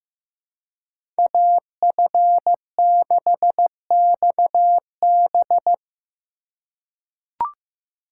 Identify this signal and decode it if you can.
Morse code, heard as a keyed tone, about 16 wpm, 700 Hz. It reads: AF6XB
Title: USA Call Signs - 5 Characters - 3x Character Spacing - 15wpm